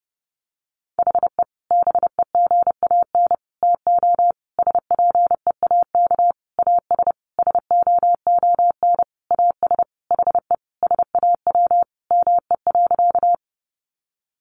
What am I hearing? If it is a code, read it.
Morse code, heard as a keyed tone, about 30 wpm, 700 Hz. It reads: HE BEGAN TO SPEAK AS SOON AS HE SAW ME.
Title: Sentences from Top 1100 Words 30wpm